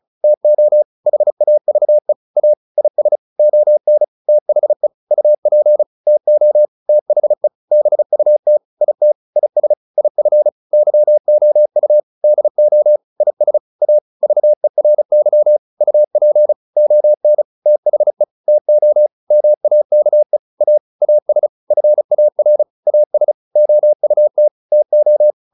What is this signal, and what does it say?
Morse code, heard as a keyed tone, about 34 wpm, 600 Hz. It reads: TO HAVE A IS ON THE UP TO THE BUT IT IS IF YOU DO IS A VERY UP ON THE TO MAKE A AS FAR AS OUT TO
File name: WD202 35